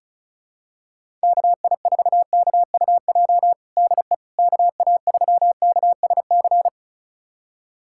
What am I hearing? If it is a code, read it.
Morse code, heard as a keyed tone, about 36 wpm, 700 Hz. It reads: KI4KUJ DE KA3KSC
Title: USA Callsign DE Callsign 35wpm